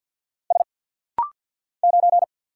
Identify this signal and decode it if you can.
Morse code, heard as a keyed tone, about 48 wpm, 700 Hz. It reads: S 9
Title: Single Letter-Number - Rapid-Fire 50wpm